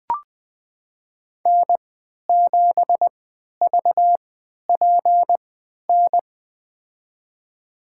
Morse code, heard as a keyed tone, about 20 wpm, spaced as for 11 wpm, 700 Hz. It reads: N7VPN
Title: USA Call Signs - 5 Characters - 3x Character Spacing - 20wpm